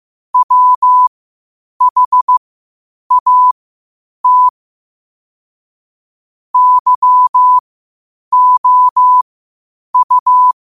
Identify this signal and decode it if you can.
Morse code, heard as a keyed tone, about 15 wpm, spaced as for 8 wpm, 1000 Hz. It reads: WHAT YOU
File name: 8wpm